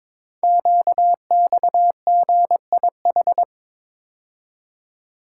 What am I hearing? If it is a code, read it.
Morse code, heard as a keyed tone, about 22 wpm, 700 Hz. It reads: QXGIH
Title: Code Groups - 5 Letters - 22wpm